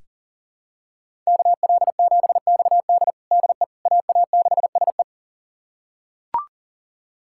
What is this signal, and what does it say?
Morse code, heard as a keyed tone, about 37 wpm, 700 Hz. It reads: KL7XD DE AA6SE
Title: USA Callsign DE Callsign - 5 Characters - 40wpm